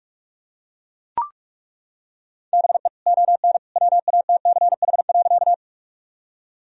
Morse code, heard as a keyed tone, about 41 wpm, 700 Hz. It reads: BE ON WATCH.
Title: Sentences from Top 1000 Words - Review Part 1 - 45wpm